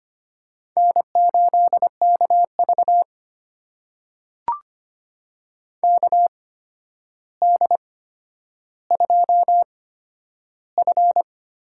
Morse code, heard as a keyed tone, about 25 wpm, spaced as for 11 wpm, 700 Hz. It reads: N8KV KD2F
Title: USA Call Signs - 8x Character Spacing - 25wpm